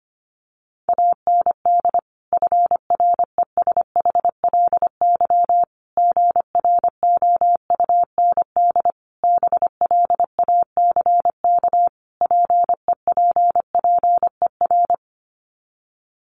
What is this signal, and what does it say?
Morse code, heard as a keyed tone, about 25 wpm, 700 Hz. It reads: AND FRESHLY GROUND BLACK PEPPER
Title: Sets of 5 Words Encore 25wpm